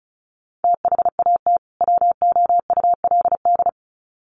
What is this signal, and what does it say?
Morse code, heard as a keyed tone, about 34 wpm, 700 Hz. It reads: THAT WOULD
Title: Sets of 2 Words 35wpm